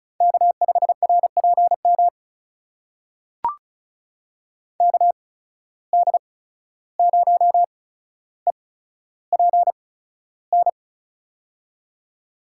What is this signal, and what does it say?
Morse code, heard as a keyed tone, about 36 wpm, spaced as for 14 wpm, 700 Hz. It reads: K5RPM KD0EPN
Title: USA Call Signs - 8x Character Spacing - 35wpm